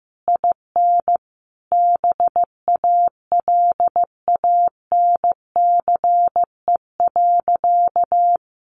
Morse code, heard as a keyed tone, about 15 wpm, 700 Hz. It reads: IN BALANCE.
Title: A Sound Of Thunder 15wpm